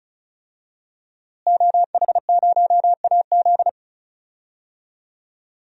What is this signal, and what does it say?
Morse code, heard as a keyed tone, about 36 wpm, 700 Hz. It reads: OH0AZ
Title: DXCC-100 Call Signs - 5 Characters - Repeated 3x - 35wpm